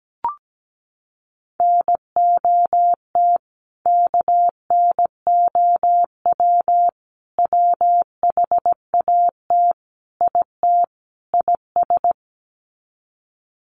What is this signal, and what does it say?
Morse code, heard as a keyed tone, about 17 wpm, 700 Hz. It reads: NOT KNOW WHAT IT IS
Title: Sets of 5 Words Encore 17wpm